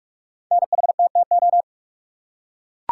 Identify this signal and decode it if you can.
Morse code, heard as a keyed tone, about 44 wpm, 700 Hz. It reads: NHTTO
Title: Code Groups - 5 Letters - 45wpm